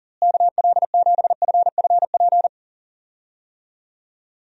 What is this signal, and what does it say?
Morse code, heard as a keyed tone, about 37 wpm, 700 Hz. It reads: KL7FFP
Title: USA Call Signs - 6 Characters - Repeated 3x - 40wpm